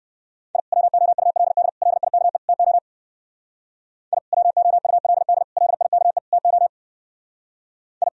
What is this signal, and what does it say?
Morse code, heard as a keyed tone, about 58 wpm, 700 Hz. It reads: I WOULD LIKE TO I WOULD LIKE TO I